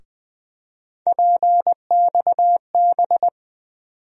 Morse code, heard as a keyed tone, about 20 wpm, 700 Hz. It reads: PXB
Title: USA Call Sign Postfixes - 3 Letters 20wpm